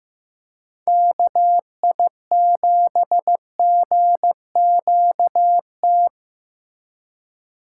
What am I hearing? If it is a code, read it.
Morse code, heard as a keyed tone, about 14 wpm, 700 Hz. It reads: KI7GQT
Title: USA Call Signs - 2x Character Spacing - 15wpm